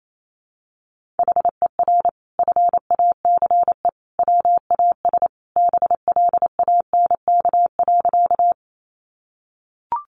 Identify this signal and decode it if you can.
Morse code, heard as a keyed tone, about 28 wpm, 700 Hz. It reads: HER FACE WAS BLANK.
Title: Sentences from Top 1500 Words 28wpm